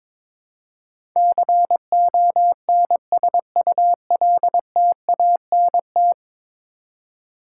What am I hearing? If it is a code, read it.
Morse code, heard as a keyed tone, about 22 wpm, 700 Hz. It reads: CONSULTANT
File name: practice / 22wpm / 10 Letter Words - First 3 Letters Given - 22wpm